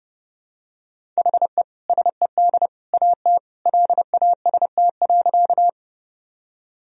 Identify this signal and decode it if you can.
Morse code, heard as a keyed tone, about 30 wpm, 700 Hz. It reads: HE SED AT LAST.
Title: Sentences from Top 300 Words and Abbreviations 30wpm